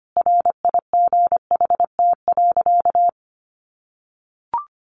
Morse code, heard as a keyed tone, about 25 wpm, 700 Hz. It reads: RIGHT.
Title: Sentences from Top 300 Words 25wpm